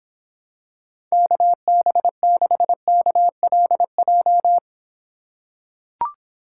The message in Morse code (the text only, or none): KB6KLJ